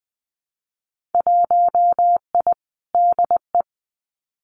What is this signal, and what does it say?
Morse code, heard as a keyed tone, about 20 wpm, 700 Hz. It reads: 1I DE
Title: Winter Field Day Exchanges Encore 20wpm